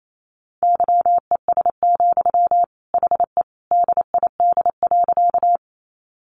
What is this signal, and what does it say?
Morse code, heard as a keyed tone, about 28 wpm, 700 Hz. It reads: YES, HE DID.
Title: Sentences from Top 500 Words 28wpm